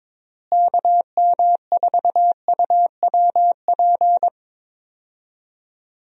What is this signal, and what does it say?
Morse code, heard as a keyed tone, about 22 wpm, 700 Hz. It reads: KM4UWP
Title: USA Call Signs - 6 Characters - Repeated 3x - 22wpm